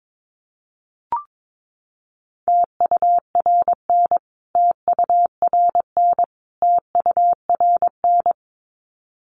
Morse code, heard as a keyed tone, about 22 wpm, 700 Hz. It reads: TURN TURN TURN
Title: Top 500 Words Repeated 22wpm